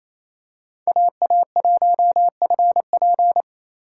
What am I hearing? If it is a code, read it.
Morse code, heard as a keyed tone, about 27 wpm, 700 Hz. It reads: AA1FP
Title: USA Call Signs - 5 Characters - 2x Character Spacing - 28wpm